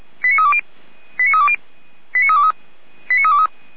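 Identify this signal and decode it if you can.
Five-tone selective call (ZVEI), heard as quick runs of five tones, tones of about 70 ms. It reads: 89320 89320 89323 89323
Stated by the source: ZVEI five-tone selective call received on an SDR.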